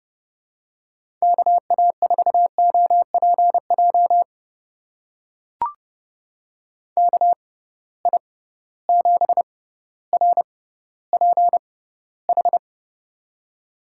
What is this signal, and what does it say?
Morse code, heard as a keyed tone, about 30 wpm, spaced as for 15 wpm, 700 Hz. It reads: KA4OPJ KI7RPH
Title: USA Call Signs - 6 Characters - 6x Character Spacing - 30wpm